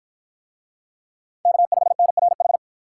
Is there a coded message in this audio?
Yes: Morse code from 1.5 to 2.6 s, heard as a keyed tone, about 52 wpm, 700 Hz. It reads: K5NRH